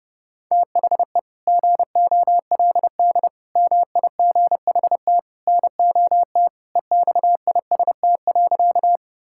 Morse code, heard as a keyed tone, about 32 wpm, 700 Hz. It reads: THE GOLD MIGHT NOT EXIST.
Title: Sentences from Top 1000 Words - Review Part 5 - 30wpm